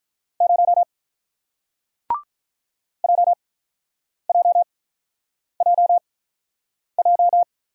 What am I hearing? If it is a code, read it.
Morse code, heard as a keyed tone, about 48 wpm, spaced as for 9 wpm, 700 Hz. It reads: 0 JJJJ